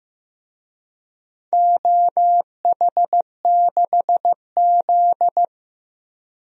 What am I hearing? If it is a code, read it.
Morse code, heard as a keyed tone, about 15 wpm, 700 Hz. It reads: OH6Z
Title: DXCC-100 Call Signs - 4 Characters - 15wpm